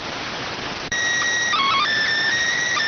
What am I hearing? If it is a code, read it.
SSTV (slow-scan television), PD 290.